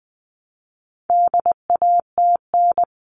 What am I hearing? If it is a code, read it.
Morse code, heard as a keyed tone, about 20 wpm, 700 Hz. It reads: DATN
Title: All Cut Numbers - 4-Digit 20wpm